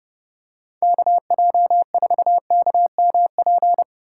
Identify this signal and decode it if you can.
Morse code, heard as a keyed tone, about 28 wpm, 700 Hz. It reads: KJ4KMP